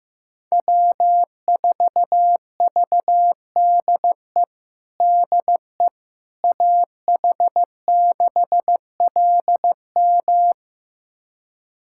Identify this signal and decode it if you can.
Morse code, heard as a keyed tone, about 15 wpm, 700 Hz. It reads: W4VDE DE AH6LM
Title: USA Callsign DE Callsign - 5 Characters - 15wpm